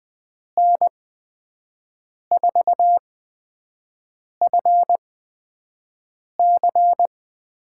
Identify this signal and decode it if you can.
Morse code, heard as a keyed tone, about 20 wpm, spaced as for 5 wpm, 700 Hz. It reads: N4FC
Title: USA Call Signs - 8x Character Spacing - 20wpm